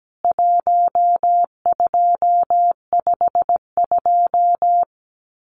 Morse code, heard as a keyed tone, about 17 wpm, 700 Hz. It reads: 1252
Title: CWT Exchange 17wpm